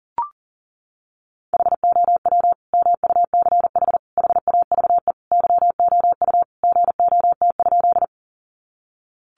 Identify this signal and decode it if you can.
Morse code, heard as a keyed tone, about 40 wpm, 700 Hz. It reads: HOW MUCH HAVE YOU GOT?
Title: Sentences from Top 300 Words 40wpm